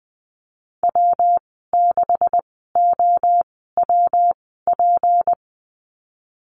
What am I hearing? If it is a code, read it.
Morse code, heard as a keyed tone, about 20 wpm, spaced as for 14 wpm, 700 Hz. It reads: W6OWP
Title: 161 Lesson 26g - Letter g - Call Signs 20wpm